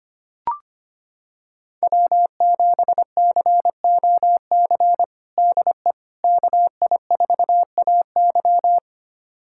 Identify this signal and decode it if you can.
Morse code, heard as a keyed tone, about 25 wpm, 700 Hz. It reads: W7COC DE KI4AY
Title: USA Callsign DE Callsign - 5 Characters - 25wpm